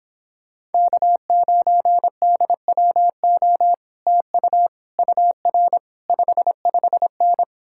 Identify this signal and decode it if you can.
Morse code, heard as a keyed tone, about 29 wpm, 700 Hz. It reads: K9DWO TU UR 55N